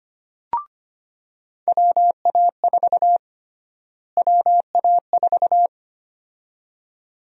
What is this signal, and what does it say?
Morse code, heard as a keyed tone, about 25 wpm, 700 Hz. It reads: WA4 WA4
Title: USA Call Sign Prefixes - Repeated 2x - 25wpm